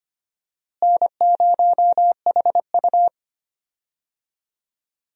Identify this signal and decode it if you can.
Morse code, heard as a keyed tone, about 25 wpm, 700 Hz. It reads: N0HU
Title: USA Call Signs - 4 Characters - Repeated 2x - 25wpm